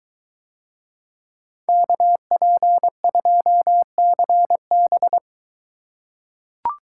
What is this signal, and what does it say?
Morse code, heard as a keyed tone, about 21 wpm, 700 Hz. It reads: KP2CB